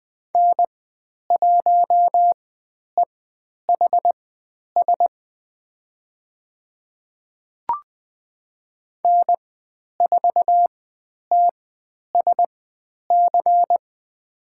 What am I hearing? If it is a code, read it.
Morse code, heard as a keyed tone, about 20 wpm, spaced as for 10 wpm, 700 Hz. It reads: N1EHS N4TSC
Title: 109 Lesson 18g - Letter m - Call Signs 20wpm